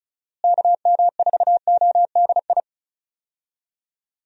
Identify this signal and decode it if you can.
Morse code, heard as a keyed tone, about 34 wpm, 700 Hz. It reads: KM4ODI
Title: USA Call Signs - 6 Characters - Repeated 2x - 35wpm